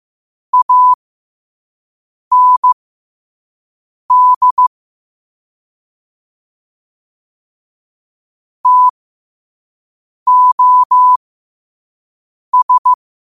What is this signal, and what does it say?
Morse code, heard as a keyed tone, about 15 wpm, spaced as for 5 wpm, 1000 Hz. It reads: AND TOS
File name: qotdfiles / 5wpm